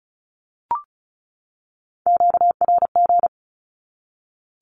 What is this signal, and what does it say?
Morse code, heard as a keyed tone, about 34 wpm, 700 Hz. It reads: QRG
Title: Q-Codes 35wpm